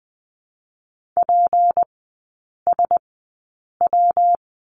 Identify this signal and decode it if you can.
Morse code, heard as a keyed tone, about 20 wpm, spaced as for 8 wpm, 700 Hz. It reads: PSW
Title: 152 Lesson 25e - Letter p - Three Characters 20wpm